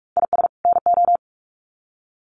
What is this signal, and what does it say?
Morse code, heard as a keyed tone, about 44 wpm, 700 Hz. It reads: IS NO
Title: Sets of 2 Words 45wpm